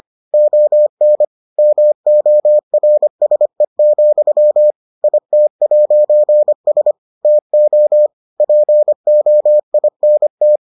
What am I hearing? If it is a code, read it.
Morse code, heard as a keyed tone, about 25 wpm, 600 Hz. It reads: ON MORSE, IT'S TO POINT